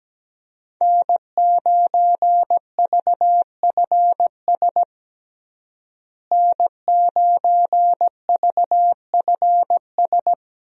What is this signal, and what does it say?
Morse code, heard as a keyed tone, about 17 wpm, 700 Hz. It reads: N9VFS N9VFS
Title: USA Call Signs - 5 Characters - Repeated 2x - 17wpm